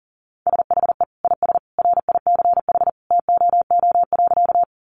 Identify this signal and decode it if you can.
Morse code, heard as a keyed tone, about 40 wpm, 700 Hz. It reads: SHE IS RICH TOO.
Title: Sentences from Top 1100 Words 40wpm